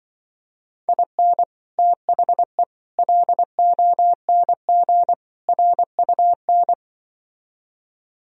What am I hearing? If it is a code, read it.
Morse code, heard as a keyed tone, about 24 wpm, 700 Hz. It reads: IN THE LONG RUN
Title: Speed-Racing - Sets of 4 Words 20wpm